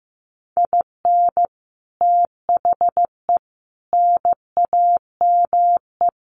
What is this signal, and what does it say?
Morse code, heard as a keyed tone, about 15 wpm, 700 Hz. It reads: IN THE NAME
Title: Sets of 4 Words 15wpm